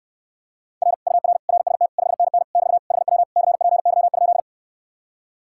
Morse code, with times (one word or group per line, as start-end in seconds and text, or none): 0.8-0.9 R
1.1-1.4 UR
1.5-1.9 RST
2.0-2.4 5NN
2.5-2.8 <BT>
2.9-3.2 HW
3.4-4.4 CPY?